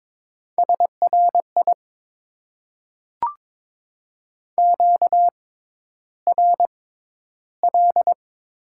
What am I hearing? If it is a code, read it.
Morse code, heard as a keyed tone, about 22 wpm, spaced as for 11 wpm, 700 Hz. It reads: SRI QRL
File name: Top 90 QSO Elements - 6x Character Spacing - 22wpm